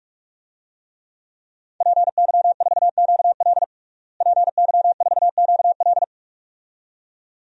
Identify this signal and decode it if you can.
Morse code, heard as a keyed tone, about 41 wpm, 700 Hz. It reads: PY4QL PY4QL
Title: DXCC-100 Call Signs - 5 Characters - Repeated 2x - 45wpm